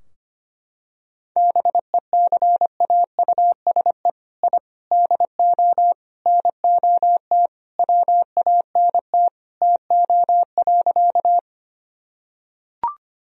Morse code, heard as a keyed tone, about 25 wpm, 700 Hz. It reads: BECAUSE I DO NOT WANT TO.
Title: Sentences from Top 1000 Words - Review Part 3 - 25wpm